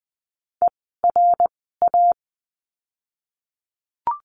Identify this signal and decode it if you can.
Morse code, heard as a keyed tone, about 20 wpm, spaced as for 15 wpm, 700 Hz. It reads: ERA
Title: 061 Lesson 11f - Letter h - Words 20wpm